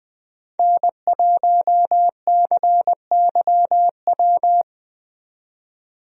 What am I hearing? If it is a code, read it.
Morse code, heard as a keyed tone, about 20 wpm, 700 Hz. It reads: N1CYW
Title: USA Call Signs - 5 Characters - 20wpm